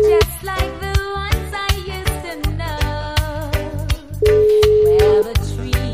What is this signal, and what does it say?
Ringback tone (Europe, 425 Hz).